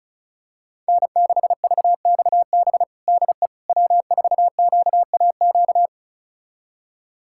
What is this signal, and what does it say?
Morse code, heard as a keyed tone, about 36 wpm, 700 Hz. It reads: N6VXB DE W4QAQ
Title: USA Callsign DE Callsign - 5 Characters - 35wpm